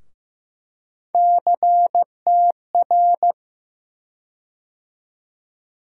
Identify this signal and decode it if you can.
Morse code, heard as a keyed tone, about 15 wpm, 700 Hz. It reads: CTR
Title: USA Call Sign Postfixes - 3 Letters 15wpm